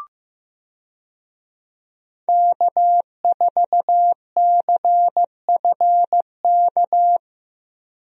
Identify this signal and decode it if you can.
Morse code, heard as a keyed tone, about 16 wpm, 700 Hz. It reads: K4CFK
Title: USA Call Signs - 5 Characters - Repeated 3x - 15wpm